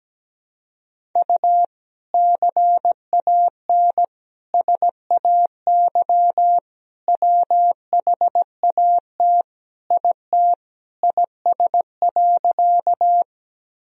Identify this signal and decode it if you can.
Morse code, heard as a keyed tone, about 17 wpm, 700 Hz. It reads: U CAN SAY WHAT IT IS.
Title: Sentences from Top 200 Words and Abbreviations 17wpm